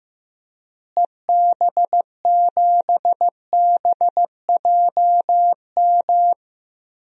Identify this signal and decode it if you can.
Morse code, heard as a keyed tone, about 15 wpm, 700 Hz. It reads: EB7BJM
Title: DXCC-100 Call Signs - 6 Characters - Repeated 3x - 15wpm